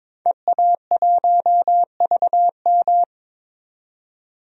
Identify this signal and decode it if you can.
Morse code, heard as a keyed tone, about 22 wpm, 700 Hz. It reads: EA1VM
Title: DXCC-100 Call Signs - 5 Characters - 22wpm